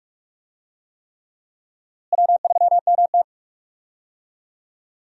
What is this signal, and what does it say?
Morse code, heard as a keyed tone, about 44 wpm, 700 Hz. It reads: W3MT
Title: USA Call Signs - 4 Characters - Repeated 3x - 45wpm